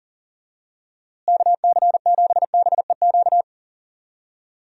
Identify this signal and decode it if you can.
Morse code, heard as a keyed tone, about 40 wpm, 700 Hz. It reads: KC7BEQ
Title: USA Call Signs - 6 Characters - Repeated 2x - 40wpm